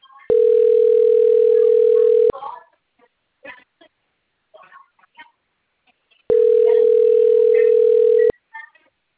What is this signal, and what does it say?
Ringback tone (North America), 440 + 480 Hz.
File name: rg-1600-unknown- - - .74087